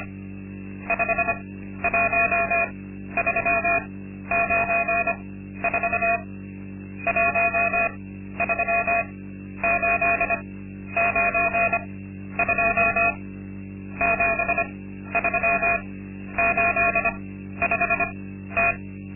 Morse code, heard as a keyed tone, about 25 wpm, spaced as for 12 wpm, 680 Hz. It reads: 51394 13892 7385T
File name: Linksmakalnis-2